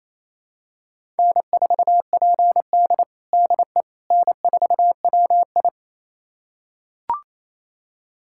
Morse code, heard as a keyed tone, about 28 wpm, 700 Hz. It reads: N4PD DE N4WI
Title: USA Callsign DE Callsign - 4 Characters - 28wpm